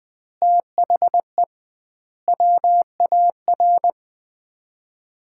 Morse code, heard as a keyed tone, about 20 wpm, 700 Hz. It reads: THE WAR
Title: 2x Word-Spacing - Sets of 5 Words - 20wpm